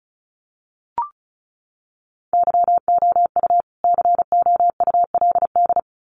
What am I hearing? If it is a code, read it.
Morse code, heard as a keyed tone, about 34 wpm, 700 Hz. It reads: YOU COULD
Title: Sets of 2 Words 35wpm